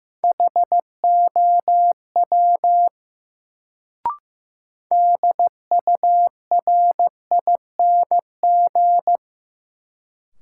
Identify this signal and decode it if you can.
Morse code, heard as a keyed tone, about 15 wpm, 700 Hz. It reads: HOW DURING
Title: Sets of 2 Words - New Word Drill - 15wpm